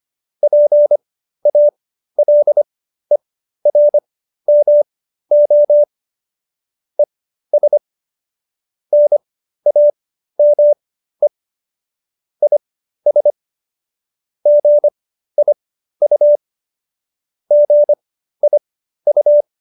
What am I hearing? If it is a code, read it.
Morse code, heard as a keyed tone, about 25 wpm, spaced as for 13 wpm, 590 Hz. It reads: PALERMO ES NAME IS GIU GIU
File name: qso215b 13